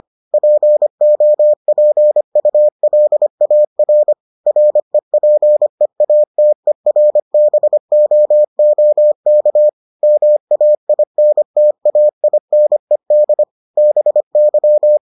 Morse code, heard as a keyed tone, about 25 wpm, 600 Hz. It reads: POPULAR REPEATERBOOK MAINTAINED BY